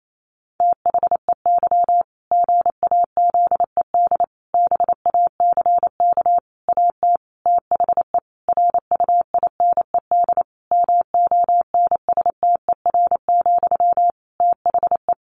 Morse code, heard as a keyed tone, about 28 wpm, 700 Hz. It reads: THEY GAZED BACK AT THE RUINED MONSTER, THE
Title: A Sound Of Thunder 28wpm